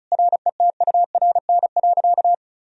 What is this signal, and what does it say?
Morse code, heard as a keyed tone, about 34 wpm, 700 Hz. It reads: RETURN.
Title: Sentences from Top 1000 Words - Review Part 1 - 35wpm